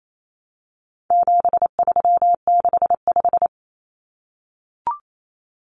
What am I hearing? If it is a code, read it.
Morse code, heard as a keyed tone, about 28 wpm, 700 Hz. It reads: 7365
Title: Common Cut Numbers - 4-Digit 28wpm